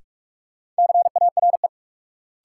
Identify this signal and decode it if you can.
Morse code, heard as a keyed tone, about 46 wpm, 700 Hz. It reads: CARE
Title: Sets of 3 Words - New Word Drill - 45wpm